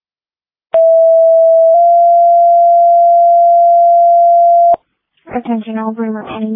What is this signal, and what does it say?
Two-tone page, heard as a one-second tone, then a three-second tone, about 660 then 690 Hz, with speech. It is heard from 0.7 to 4.7 s.